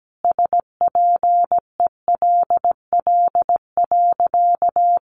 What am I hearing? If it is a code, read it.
Morse code, heard as a keyed tone, about 17 wpm, 700 Hz. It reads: SPELL.
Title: Sentences from Top 1200 Words 17wpm